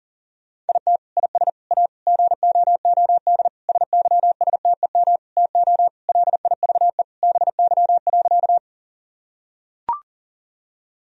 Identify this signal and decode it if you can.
Morse code, heard as a keyed tone, about 40 wpm, 700 Hz. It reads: IT IS A GOOD SYSTEM TO LIVE BY.
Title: Sentences from Top 1000 Words - Review Part 3 - 40wpm